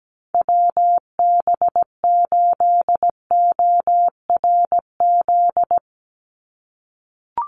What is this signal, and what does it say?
Morse code, heard as a keyed tone, about 17 wpm, 700 Hz. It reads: WB8ORZ
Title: USA Call Signs 17wpm